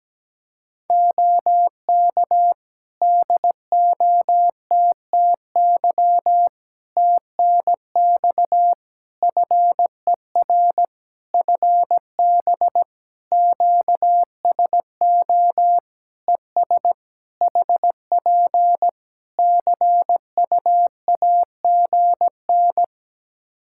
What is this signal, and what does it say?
Morse code, heard as a keyed tone, about 17 wpm, 700 Hz. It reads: OK DOTTY TNX FER FB QSO ES HP CUAGN
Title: K1USN SST QSO Practice 17wpm